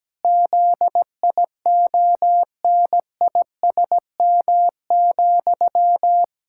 Morse code, heard as a keyed tone, about 17 wpm, 700 Hz. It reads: ZIONISM,
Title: Pangram Challenge v2 - 17wpm